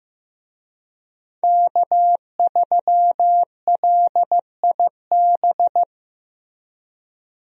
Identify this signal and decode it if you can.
Morse code, heard as a keyed tone, about 15 wpm, 700 Hz. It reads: K3LIB
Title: USA Call Signs - 5 Characters - Repeated 3x - 15wpm